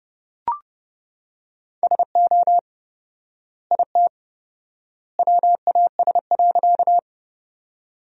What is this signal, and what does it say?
Morse code, heard as a keyed tone, about 32 wpm, 700 Hz. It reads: SO IT WAS.
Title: 4x Word-Spacing - Sentences from Top 100 Words - 30wpm